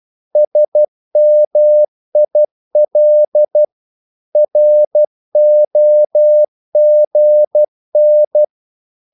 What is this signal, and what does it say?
Morse code, heard as a keyed tone, about 12 wpm, 600 Hz. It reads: SMIL ROGN
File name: Korte ord DK 12wpm